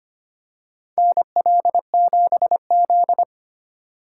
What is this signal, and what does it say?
Morse code, heard as a keyed tone, about 25 wpm, 700 Hz. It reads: NL7Z
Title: USA Call Signs - 4 Characters - 4x Character Spacing - 25wpm